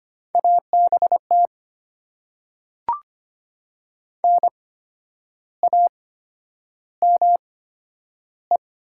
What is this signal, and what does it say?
Morse code, heard as a keyed tone, about 25 wpm, spaced as for 10 wpm, 700 Hz. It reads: ABT NAME
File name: Top 90 QSO Elements - 8x Character Spacing - 25wpm